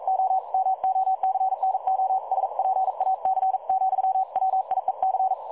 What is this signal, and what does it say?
Morse code, heard as a keyed tone, about 42 wpm, 760 Hz. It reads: O MOCNO SPAC , WIEC
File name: dlazaawansowanych / nagraniazpasma / websdr recording 3565.0kHz